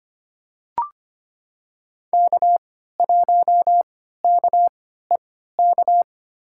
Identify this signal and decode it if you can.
Morse code, heard as a keyed tone, about 25 wpm, spaced as for 14 wpm, 700 Hz. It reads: K1KEK
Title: USA Call Signs - 3x Character Spacing - 25wpm